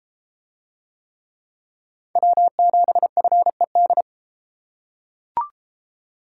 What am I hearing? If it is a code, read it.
Morse code, heard as a keyed tone, about 34 wpm, 700 Hz. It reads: W7FED